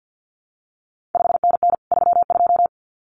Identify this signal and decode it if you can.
Morse code, heard as a keyed tone, about 50 wpm, 700 Hz. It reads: 5NN 32
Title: CQ WW DX Contest Exchange - 50wpm